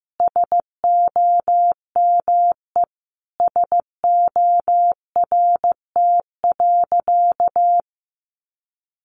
Morse code, heard as a keyed tone, about 15 wpm, 700 Hz. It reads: SOME SORT.
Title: Sentences from Top 1300 Words 15wpm